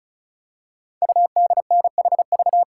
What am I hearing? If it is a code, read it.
Morse code, heard as a keyed tone, about 36 wpm, 690 Hz. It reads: UDNHV
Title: Code Groups - 5 Letters - 35wpm